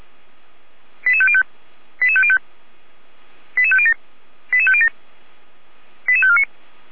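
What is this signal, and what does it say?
Five-tone selective call (ZVEI), heard as quick runs of five tones, tones of about 70 ms. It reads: 88505 88505 88507 88507 88549
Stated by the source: ZVEI five-tone selective call received on an SDR.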